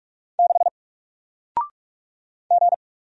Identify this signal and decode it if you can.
Morse code, heard as a keyed tone, about 46 wpm, 700 Hz. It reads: 6 G
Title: Single Letter-Number Alt 45wpm